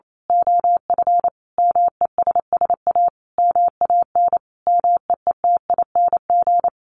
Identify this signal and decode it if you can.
Morse code, heard as a keyed tone, about 28 wpm, 700 Hz. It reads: OF MESS A MAN MEETING
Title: A Sound Of Thunder 28wpm